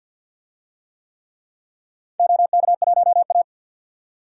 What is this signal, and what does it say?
Morse code, heard as a keyed tone, about 51 wpm, 700 Hz. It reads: OK1A
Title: DXCC-100 Call Signs - 4 Characters - 50wpm